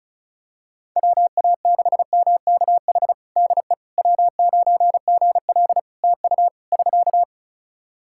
Morse code, heard as a keyed tone, about 36 wpm, 700 Hz. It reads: WA6MKH DE W9GL TU <SK>
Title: Practice QSO - Exchange 4 - Ending - 35wpm